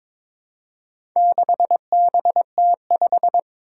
Morse code, heard as a keyed tone, about 22 wpm, 700 Hz. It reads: 6BT5
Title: Four Challenging Letter-Number Combinations 22wpm